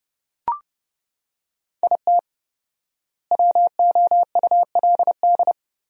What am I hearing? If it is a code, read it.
Morse code, heard as a keyed tone, about 32 wpm, 700 Hz. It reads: IT WOULD
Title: 4x Word-Spacing - Sentences from Top 400 Words - 30wpm